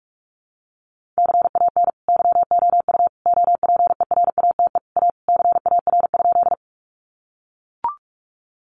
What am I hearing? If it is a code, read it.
Morse code, heard as a keyed tone, about 44 wpm, 700 Hz. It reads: CAN YOU OPERATE A CAR?
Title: Sentences from Top 1500 Words 45wpm